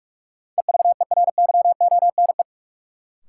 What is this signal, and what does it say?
Morse code, heard as a keyed tone, about 42 wpm, 700 Hz. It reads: EVERYONE
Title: Sentences from Top 800 Words - New Word Drill - 45wpm